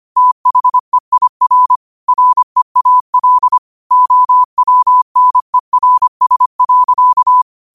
Morse code, heard as a keyed tone, about 25 wpm, 1000 Hz. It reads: THEIR REAL OWNERS.
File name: qotdfiles / 25wpm